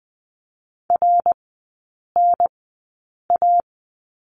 Morse code, heard as a keyed tone, about 20 wpm, spaced as for 8 wpm, 690 Hz. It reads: RNA
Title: 081 Lesson 14e - Number 2 - Three Characters 20wpm